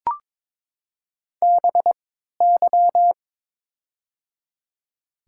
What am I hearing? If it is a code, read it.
Morse code, heard as a keyed tone, about 22 wpm, spaced as for 13 wpm, 700 Hz. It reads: BY
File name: Top 90 QSO Elements - 3x Character Spacing - 22wpm